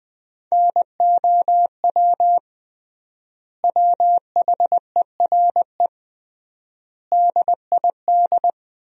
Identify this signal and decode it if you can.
Morse code, heard as a keyed tone, about 21 wpm, 700 Hz. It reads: NOW WHERE DID
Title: 3x Word-Spacing - Sentences from Top 200 Words - 20wpm